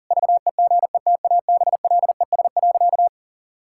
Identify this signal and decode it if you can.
Morse code, heard as a keyed tone, about 40 wpm, 700 Hz. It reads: VEGETABLES.
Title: Pangram Challenge v2 - 40wpm